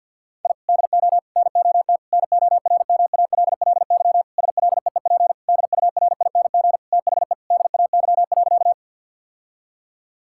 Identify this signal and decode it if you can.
Morse code, heard as a keyed tone, about 49 wpm, 700 Hz. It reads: I DO NOT NORMALLY SLEEP DURING THE DAY.
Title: Sentences from Top 1000 Words - Review Part 6 - 50wpm